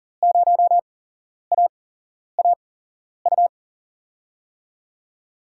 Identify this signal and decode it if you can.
Morse code, heard as a keyed tone, about 38 wpm, spaced as for 11 wpm, 700 Hz. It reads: 0AAU
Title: USA Call Signs - 6 Characters - 8x Character Spacing - 40wpm